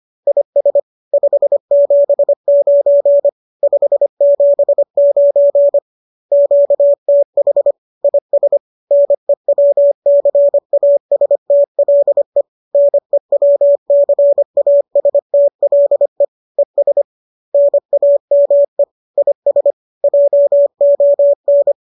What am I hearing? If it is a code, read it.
Morse code, heard as a keyed tone, about 25 wpm, 570 Hz. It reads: IS 579 579 QTH IS NEWCASTLE NEWCASTLE ES NAME IS JON